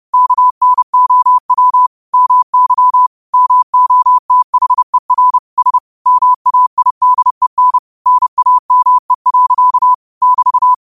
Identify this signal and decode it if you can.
Morse code, heard as a keyed tone, about 30 wpm, 1000 Hz. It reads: KNOW MY MOTHER S MAIDEN NAME. <BT>
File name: qotdfiles / 30wpm